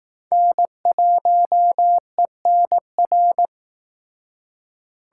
Morse code, heard as a keyed tone, about 19 wpm, 700 Hz. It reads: N1ENR
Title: Speed Racing - USA 1x3 Call Signs 15wpm